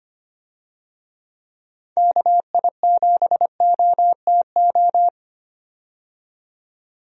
Morse code, heard as a keyed tone, about 25 wpm, 700 Hz. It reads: KI7OTO
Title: USA Call Signs - 6 Characters - 25wpm